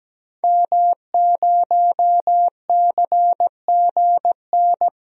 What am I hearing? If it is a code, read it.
Morse code, heard as a keyed tone, about 17 wpm, 700 Hz. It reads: M0CGN
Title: DXCC-100 Call Signs - 5 Characters - Repeated 3x - 17wpm